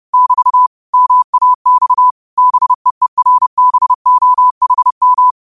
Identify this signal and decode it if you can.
Morse code, heard as a keyed tone, about 30 wpm, 1000 Hz. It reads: <BT> MAX BEERBOHM
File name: qotdfiles / 30wpm